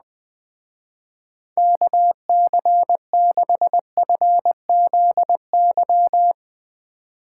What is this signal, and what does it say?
Morse code, heard as a keyed tone, about 20 wpm, 700 Hz. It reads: KC6FZY